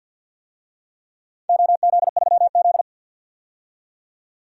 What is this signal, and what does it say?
Morse code, heard as a keyed tone, about 47 wpm, 700 Hz. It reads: OZ3Z
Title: DXCC-100 Call Signs - 4 Characters - 50wpm